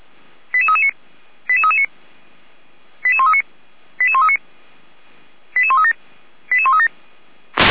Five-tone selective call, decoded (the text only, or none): 88339 88339 88139 88139 88137 88137